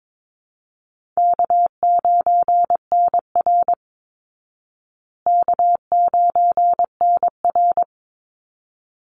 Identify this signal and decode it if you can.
Morse code, heard as a keyed tone, about 22 wpm, 690 Hz. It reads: K9NR K9NR
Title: Easy Call Signs Repeated 2x - 22wpm